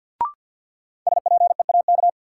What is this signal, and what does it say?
Morse code, heard as a keyed tone, about 52 wpm, 700 Hz. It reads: SPEAK
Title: Sentences from Top 1100 Words - New Word Drill Alt - 50wpm